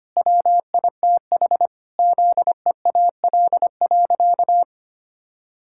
Morse code, heard as a keyed tone, about 25 wpm, 700 Hz. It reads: WITH ZEAL.
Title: Pangram Challenge v2 - 25wpm